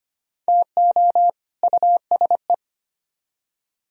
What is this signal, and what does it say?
Morse code, heard as a keyed tone, about 32 wpm, 700 Hz. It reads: TO USE